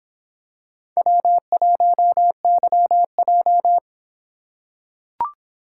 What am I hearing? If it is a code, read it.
Morse code, heard as a keyed tone, about 26 wpm, 700 Hz. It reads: W1YJ